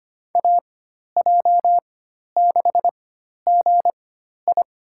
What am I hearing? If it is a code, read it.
Morse code, heard as a keyed tone, about 25 wpm, spaced as for 12 wpm, 700 Hz. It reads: AJ6GI
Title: USA Call Signs - 4x Character Spacing - 25wpm